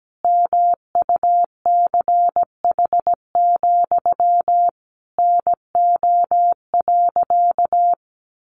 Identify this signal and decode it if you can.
Morse code, heard as a keyed tone, about 17 wpm, 700 Hz. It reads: MUCH, NO.
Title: A Sound Of Thunder 17wpm